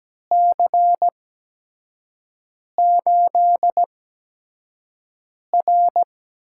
Morse code, heard as a keyed tone, about 17 wpm, spaced as for 5 wpm, 700 Hz. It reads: C8R
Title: USA Call Signs - 6 Characters - 8x Character Spacing - 17wpm